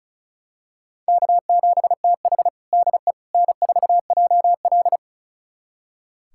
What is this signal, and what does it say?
Morse code, heard as a keyed tone, about 36 wpm, 700 Hz. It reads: K7TH DE N4JL
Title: USA Callsign DE Callsign - 4 Characters - 35wpm